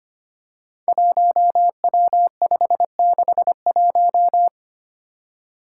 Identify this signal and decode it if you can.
Morse code, heard as a keyed tone, about 25 wpm, 700 Hz. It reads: 1W561
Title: Five Challenging Letter-Number Combinations 25wpm